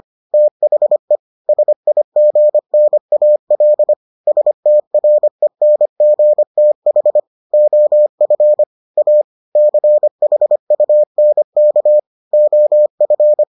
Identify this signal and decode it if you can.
Morse code, heard as a keyed tone, about 25 wpm, 600 Hz. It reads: THE SIGNAL STRENGTH OF A CHUNK OF